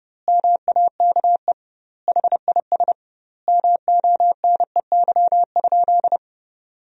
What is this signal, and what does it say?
Morse code, heard as a keyed tone, about 30 wpm, 700 Hz. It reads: MAKE HIS MONEY?
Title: 2x Word-Spacing - Sentences from Top 400 Words - 30wpm